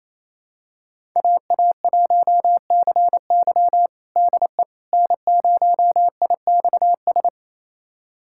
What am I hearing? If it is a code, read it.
Morse code, heard as a keyed tone, about 28 wpm, 700 Hz. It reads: AA1CY DE N0IXS
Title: USA Callsign DE Callsign - 5 Characters - 28wpm